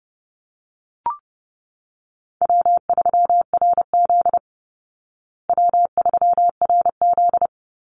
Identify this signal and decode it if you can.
Morse code, heard as a keyed tone, about 30 wpm, 700 Hz. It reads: W3RZ W3RZ
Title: Easy Call Signs Repeated 3x - 30wpm